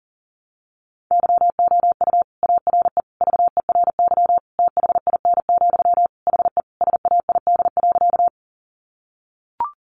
Morse code, heard as a keyed tone, about 37 wpm, 700 Hz. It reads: YOU ARE VERY THIN, HE SAID.
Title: Sentences from Top 1200 Words 40wpm